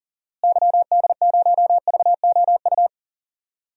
Morse code, heard as a keyed tone, about 40 wpm, 700 Hz. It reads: YD0VOU
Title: DXCC-100 Call Signs - 6 Characters - Repeated 3x - 40wpm